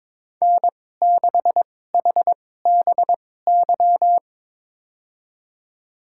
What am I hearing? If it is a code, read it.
Morse code, heard as a keyed tone, about 22 wpm, spaced as for 16 wpm, 700 Hz. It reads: N6HBY
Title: USA Call Signs - 2x Character Spacing - 22wpm